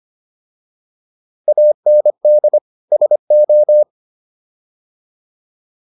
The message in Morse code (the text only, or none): AND SO